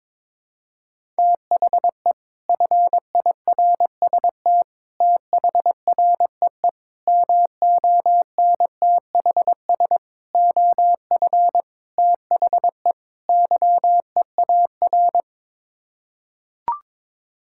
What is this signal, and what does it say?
Morse code, heard as a keyed tone, about 23 wpm, 700 Hz. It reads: THE FIRST THREE MONTHS OF THE YEAR
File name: practice / 22wpm / Sets of 7 Words - 22wpm